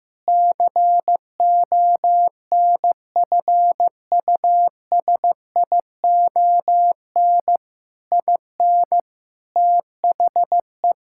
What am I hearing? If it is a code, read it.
Morse code, heard as a keyed tone, about 15 wpm, 700 Hz. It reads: CONFUSION IN THE
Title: Sentences from Top 1600 Words 15wpm